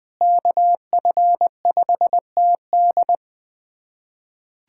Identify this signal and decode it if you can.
Morse code, heard as a keyed tone, about 20 wpm, 700 Hz. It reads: KF5TD
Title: USA Callsign DE Callsign - 5 Characters - 20wpm